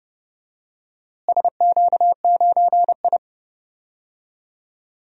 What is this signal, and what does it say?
Morse code, heard as a keyed tone, about 32 wpm, 700 Hz. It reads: SQ9I
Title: DXCC-100 Call Signs - 4 Characters - Repeated 3x - 30wpm